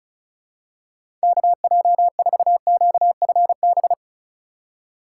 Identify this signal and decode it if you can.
Morse code, heard as a keyed tone, about 36 wpm, 700 Hz. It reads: KJ4QFB
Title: USA Call Signs - 6x Character Spacing - 35wpm